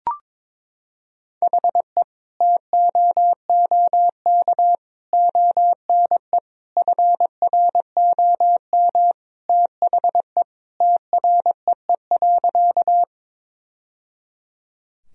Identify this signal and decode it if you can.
Morse code, heard as a keyed tone, about 22 wpm, 700 Hz. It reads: HE TOOK ONE FROM THE TREE.
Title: Sentences from Top 1000 Words - Review Part 1 - 22wpm